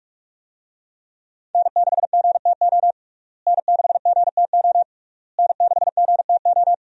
Morse code, heard as a keyed tone, about 45 wpm, 700 Hz. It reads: N6GTO N6GTO N6GTO
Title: USA Call Signs - 5 Characters - Repeated 3x - 45wpm